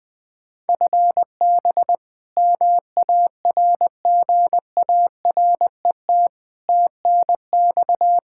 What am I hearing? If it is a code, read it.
Morse code, heard as a keyed tone, about 20 wpm, 700 Hz. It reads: FB MARGARET TNX
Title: Practice QSO - Exchange 3 - More Info - 20wpm